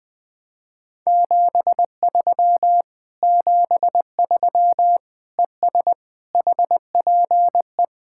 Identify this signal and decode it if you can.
Morse code, heard as a keyed tone, about 20 wpm, 700 Hz. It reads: 73 73 ES HPE
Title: Speed-Racing Common QSO Phrases Encore - 17wpm